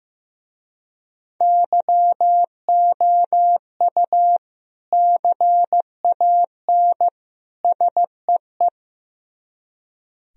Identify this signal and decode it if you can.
Morse code, heard as a keyed tone, about 15 wpm, 700 Hz. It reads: YOU CAN SEE
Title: Speed-Racing - Sets of 3 Words 15wpm